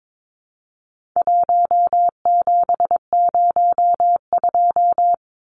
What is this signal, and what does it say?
Morse code, heard as a keyed tone, about 22 wpm, 700 Hz. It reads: 1702
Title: Numbers - 4-Digit 22wpm